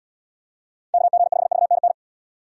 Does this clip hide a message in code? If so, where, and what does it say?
Yes: Morse code from 0.9 to 1.9 s, heard as a keyed tone, about 70 wpm, 700 Hz. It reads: KG5VNN